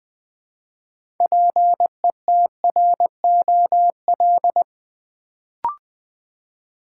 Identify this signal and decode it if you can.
Morse code, heard as a keyed tone, about 20 wpm, 700 Hz. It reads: PETROL
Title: Build A Word - 2 Letters At A Time - 6-Letter Words - 20wpm